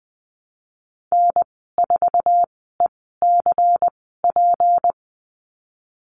Morse code, heard as a keyed tone, about 20 wpm, spaced as for 15 wpm, 700 Hz. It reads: N4ECP
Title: 173 Lesson 28g - Number 9 - Call Signs 20wpm